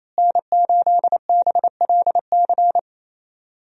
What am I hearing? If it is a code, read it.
Morse code, heard as a keyed tone, about 27 wpm, 700 Hz. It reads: N8BLC
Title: USA Call Signs - 5 Characters - Repeated 2x - 28wpm